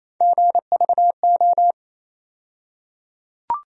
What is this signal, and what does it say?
Morse code, heard as a keyed tone, about 27 wpm, 700 Hz. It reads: GVO
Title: USA Call Sign Postfixes - 3 Letters 28wpm